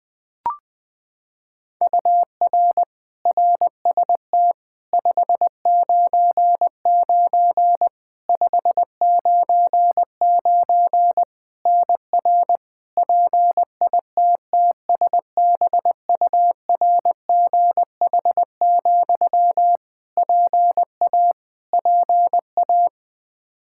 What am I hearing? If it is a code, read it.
Morse code, heard as a keyed tone, about 20 wpm, 700 Hz. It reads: UR RST 599 599 NR PITTSBURGH, PA PA
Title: Practice QSO - Exchange 1 - Beginning - 20wpm